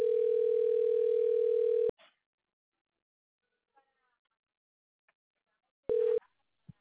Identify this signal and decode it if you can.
Ringback tone (North America), 440 + 480 Hz.